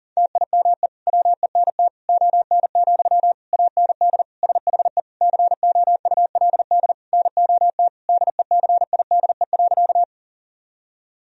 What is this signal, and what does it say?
Morse code, heard as a keyed tone, about 40 wpm, 700 Hz. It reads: TIME WENT ON, AND SHE COULD NOT DECIDE.
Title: Sentences from Top 1000 Words - Review Part 3 - 40wpm